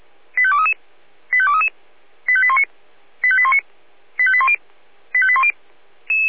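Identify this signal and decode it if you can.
Five-tone selective call (ZVEI), heard as quick runs of five tones, tones of about 70 ms. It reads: 85233 85233 85819 85819 85810 85810
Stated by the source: ZVEI five-tone selective call received on an SDR.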